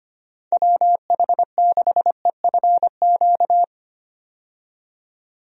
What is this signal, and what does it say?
Morse code, heard as a keyed tone, about 25 wpm, 700 Hz. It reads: WH6EFQ
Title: USA Call Signs - 6 Characters - 25wpm